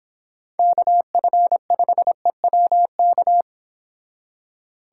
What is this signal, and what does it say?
Morse code, heard as a keyed tone, about 30 wpm, 700 Hz. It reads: KF5EWK